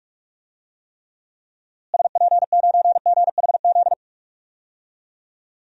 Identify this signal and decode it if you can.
Morse code, heard as a keyed tone, about 47 wpm, 700 Hz. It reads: SP9GHZ
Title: DXCC-100 Call Signs - 6 Characters - Repeated 2x - 45wpm